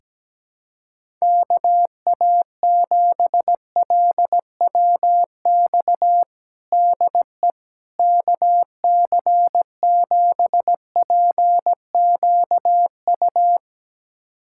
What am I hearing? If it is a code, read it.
Morse code, heard as a keyed tone, about 17 wpm, 700 Hz. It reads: KA7LWX DE KC7PQU
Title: USA Callsign DE Callsign - 6 Characters - 17wpm